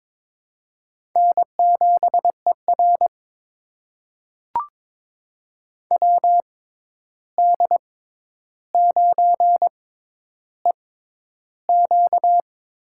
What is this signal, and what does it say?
Morse code, heard as a keyed tone, about 22 wpm, spaced as for 11 wpm, 700 Hz. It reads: N7ER WD9EQ
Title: USA Call Signs - 6x Character Spacing - 22wpm